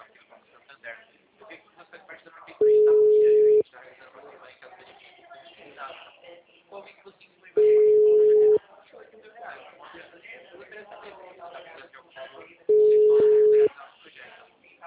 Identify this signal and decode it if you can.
Ringback tone (Europe, 425 Hz), with speech.